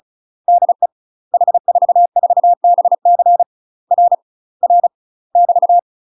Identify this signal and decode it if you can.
Morse code, heard as a keyed tone, about 36 wpm, 700 Hz. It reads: DE H44BC R R <BT>
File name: UK MASTER QSOs TRAINING / 35WPM / UK MASTER QSOs 0018